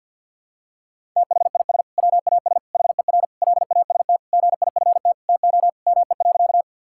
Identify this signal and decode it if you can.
Morse code, heard as a keyed tone, about 51 wpm, 700 Hz. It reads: THIS WAS HER LAST GIFT TO ME.
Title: Sentences from Top 1000 Words - Review Part 6 - 50wpm